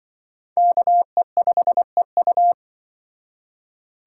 Morse code, heard as a keyed tone, about 25 wpm, 700 Hz. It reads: KE5EU
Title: Speed Racing - USA 2x2 Call Signs 20wpm